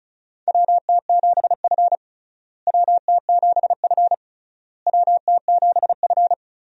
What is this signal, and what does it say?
Morse code, heard as a keyed tone, about 36 wpm, 700 Hz. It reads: WT7F WT7F WT7F
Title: USA Call Signs - 4 Characters - Repeated 3x - 35wpm